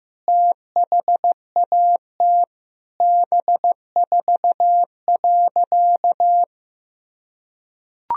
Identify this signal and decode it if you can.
Morse code, heard as a keyed tone, about 15 wpm, 700 Hz. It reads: THAT B4.
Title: Sentences from Top 400 Words and Abbreviations 15wpm